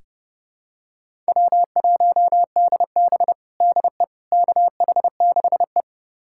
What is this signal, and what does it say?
Morse code, heard as a keyed tone, about 30 wpm, 700 Hz. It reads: W1DB DE KH6E
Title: USA Callsign DE Callsign - 4 Characters - 30wpm